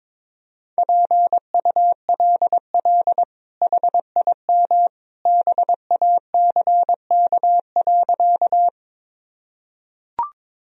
Morse code, heard as a keyed tone, about 22 wpm, 700 Hz. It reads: PULL HIM BACK.
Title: Sentences from Top 1000 Words - Review Part 2 - 22wpm